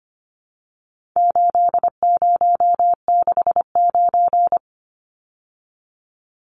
Morse code, heard as a keyed tone, about 25 wpm, 700 Hz. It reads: 8069
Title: Numbers - 4-Digit 25wpm